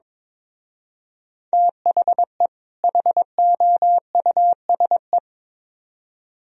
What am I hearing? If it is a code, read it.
Morse code, heard as a keyed tone, about 24 wpm, 700 Hz. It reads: THE HOUSE
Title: Speed-Racing - Sets of 2 Words 22wpm